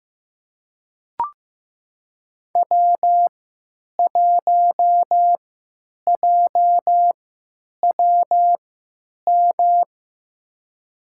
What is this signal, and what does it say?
Morse code, heard as a keyed tone, about 15 wpm, spaced as for 9 wpm, 700 Hz. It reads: W1JWM
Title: USA Call Signs - 5 Characters - 3x Character Spacing - 15wpm